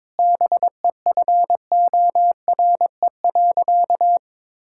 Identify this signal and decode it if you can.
Morse code, heard as a keyed tone, about 22 wpm, 700 Hz. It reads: BEFORE.
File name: Sentences with 9-Word Phrases - 22wpm